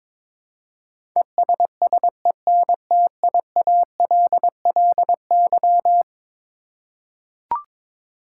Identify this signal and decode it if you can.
Morse code, heard as a keyed tone, about 22 wpm, 700 Hz. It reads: ESSENTIALLY
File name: practice / 22wpm / Eleven Letter Words - 22wpm